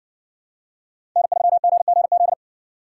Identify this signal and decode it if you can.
Morse code, heard as a keyed tone, about 54 wpm, 700 Hz. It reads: N3GGZ